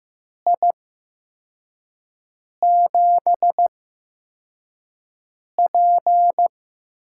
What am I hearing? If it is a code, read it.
Morse code, heard as a keyed tone, about 15 wpm, spaced as for 4 wpm, 700 Hz. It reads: I7P
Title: USA Call Signs - 4 Characters - 8x Character Spacing - 15wpm